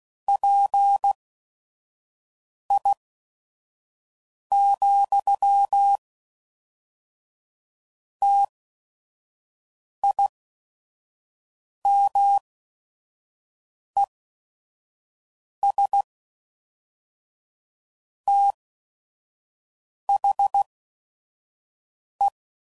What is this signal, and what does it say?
Morse code, heard as a keyed tone, about 16 wpm, spaced as for 5 wpm, 800 Hz. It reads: PI, TIMES THE
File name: CW Code Groups @5WPM / Technical Subjects / XC